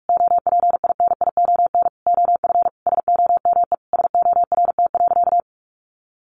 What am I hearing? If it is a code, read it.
Morse code, heard as a keyed tone, about 44 wpm, 700 Hz. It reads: OPINION OF SOME SORT.
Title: Sentences from Top 1300 Words 45wpm